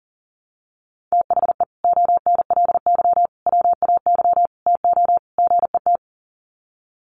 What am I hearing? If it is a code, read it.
Morse code, heard as a keyed tone, about 40 wpm, 700 Hz. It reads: THE ONLY WAY TO GET
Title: Sets of 5 Words Encore2 40wpm